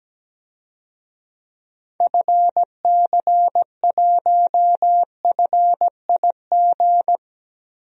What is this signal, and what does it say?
Morse code, heard as a keyed tone, about 17 wpm, 700 Hz. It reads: FC1FIG
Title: DXCC-100 Call Signs - 6 Characters - 17wpm